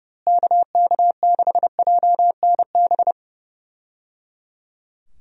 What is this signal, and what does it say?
Morse code, heard as a keyed tone, about 31 wpm, 700 Hz. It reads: KK6JNB
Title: USA Call Signs - 6 Characters - Repeated 3x - 30wpm